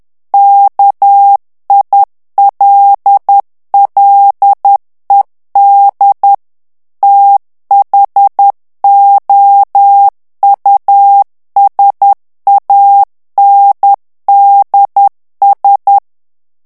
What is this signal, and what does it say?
Morse code, heard as a keyed tone, about 10 wpm, 800 Hz. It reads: KILLED THOUSANDS
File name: ANGLAIS texte clair 2 vitesse 10 mots minute